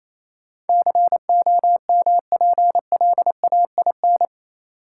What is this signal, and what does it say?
Morse code, heard as a keyed tone, about 28 wpm, 700 Hz. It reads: COMPLAIN
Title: Sentences from Top 1300 Words - New Word Drill Alt - 28wpm